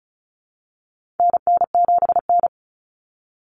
Morse code, heard as a keyed tone, about 34 wpm, 700 Hz. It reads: NN7N
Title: Common Cut Numbers - 4-Digit 35wpm